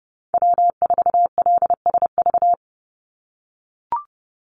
Morse code, heard as a keyed tone, about 30 wpm, 700 Hz. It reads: W4LSV
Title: Call Signs - 30wpm